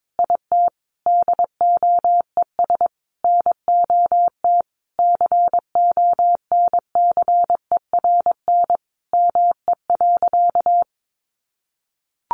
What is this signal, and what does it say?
Morse code, heard as a keyed tone, about 22 wpm, 700 Hz. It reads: IT DOES NOT CONCERN ME.
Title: Sentences from Top 1000 Words 22wpm